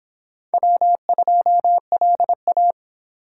Morse code, heard as a keyed tone, about 24 wpm, 700 Hz. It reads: W2LA